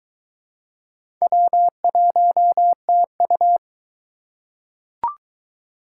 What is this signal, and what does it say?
Morse code, heard as a keyed tone, about 22 wpm, 700 Hz. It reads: W1TU